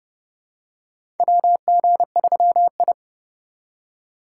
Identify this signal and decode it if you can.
Morse code, heard as a keyed tone, about 30 wpm, 700 Hz. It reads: WG3I
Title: USA Call Signs - 4 Characters - 4x Character Spacing - 30wpm